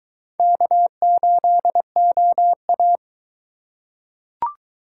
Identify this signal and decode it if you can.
Morse code, heard as a keyed tone, about 21 wpm, 700 Hz. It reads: K8OA